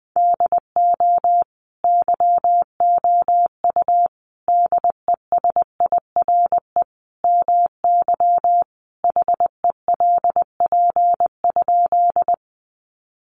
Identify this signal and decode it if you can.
Morse code, heard as a keyed tone, about 20 wpm, 700 Hz. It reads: DO YOU DESIRE MY HELP?
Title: Sentences from Top 1100 Words 20wpm